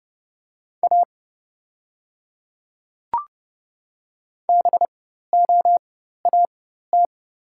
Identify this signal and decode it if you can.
Morse code, heard as a keyed tone, about 30 wpm, spaced as for 14 wpm, 690 Hz. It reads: A BOAT
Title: 4x Character Spacing - Top 500 Words - 30wpm